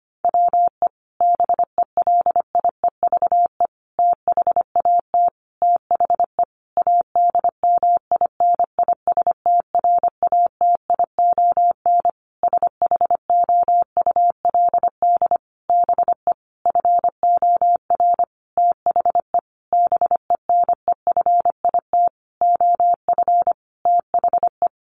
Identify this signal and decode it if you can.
Morse code, heard as a keyed tone, about 25 wpm, 700 Hz. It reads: WE BELIEVE THAT THE ADMINISTRATION SHOULD BE FOR THE BENEFIT OF THE
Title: Sentences from Top 1300 Words 25wpm